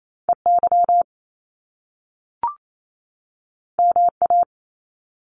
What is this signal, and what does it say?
Morse code, heard as a keyed tone, about 28 wpm, 700 Hz. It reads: EY MA
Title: Top Two Letter Combinations in Words 28wpm